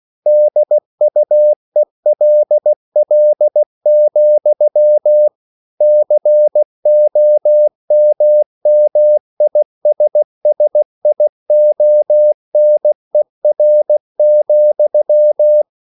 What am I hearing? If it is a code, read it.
Morse code, heard as a keyed tone, about 16 wpm, 600 Hz. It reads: DUELL, COMMISSIONER,
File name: CW / Practice / EnglishQuotations0416